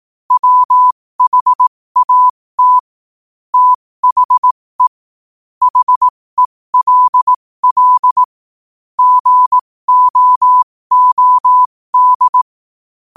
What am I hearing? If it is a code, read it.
Morse code, heard as a keyed tone, about 18 wpm, spaced as for 15 wpm, 1000 Hz. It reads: WHAT THE HELL GOOD
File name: qotdfiles / 15wpm